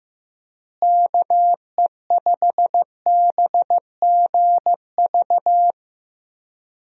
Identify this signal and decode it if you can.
Morse code, heard as a keyed tone, about 15 wpm, 700 Hz. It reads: KE5BGV
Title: USA Call Signs - 6 Characters - Repeated 2x - 15wpm